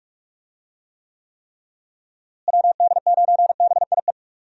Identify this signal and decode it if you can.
Morse code, heard as a keyed tone, about 46 wpm, 700 Hz. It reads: WD9BIE
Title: USA Call Signs - 6 Characters - Repeated 3x - 45wpm